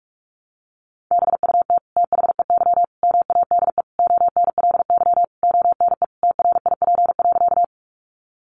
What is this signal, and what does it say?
Morse code, heard as a keyed tone, about 44 wpm, 700 Hz. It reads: BUT THEY MADE ONLY ONE TRIP.
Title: Sentences from Top 1300 Words 45wpm